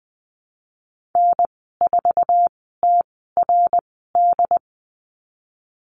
Morse code, heard as a keyed tone, about 20 wpm, spaced as for 15 wpm, 700 Hz. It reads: N4TRD
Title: 069 Lesson 12g - Letter d - Call Signs 20wpm